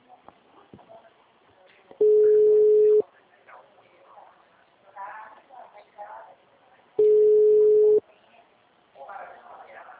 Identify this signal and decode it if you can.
Ringback tone (Europe, 425 Hz), with speech.